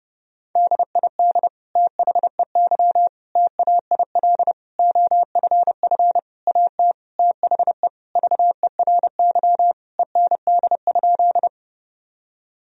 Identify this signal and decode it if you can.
Morse code, heard as a keyed tone, about 31 wpm, 700 Hz. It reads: DID THEY TAIL OFF AT THE VERY END?
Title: Sentences from Top 1000 Words - Review Part 3 - 30wpm